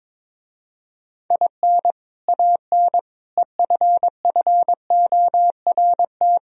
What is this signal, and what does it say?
Morse code, heard as a keyed tone, about 23 wpm, 700 Hz. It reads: IN AN EFFORT
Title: Speed-Racing - Sets of 4 Words 22wpm